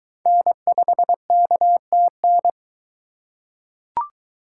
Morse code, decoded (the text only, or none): N5KTN